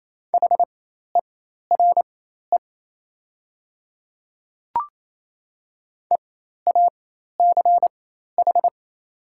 Morse code, heard as a keyed tone, about 28 wpm, spaced as for 13 wpm, 700 Hz. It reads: HERE EACH
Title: 4x Character Spacing - Top 300 Words - 28wpm